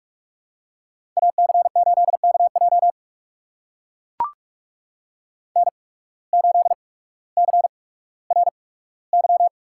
Morse code, heard as a keyed tone, about 44 wpm, spaced as for 19 wpm, 700 Hz. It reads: AC8KJ N8CRY
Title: USA Call Signs - 8x Character Spacing - 45wpm